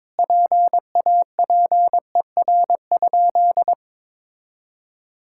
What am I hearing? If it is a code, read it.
Morse code, heard as a keyed tone, about 22 wpm, 700 Hz. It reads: PAPER?
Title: Sentences from Top 1600 Words 22wpm